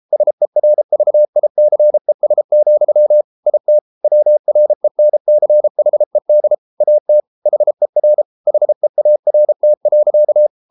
Morse code, heard as a keyed tone, about 33 wpm, 600 Hz. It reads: SERVICES, IT WRENCHED AT HER HEART.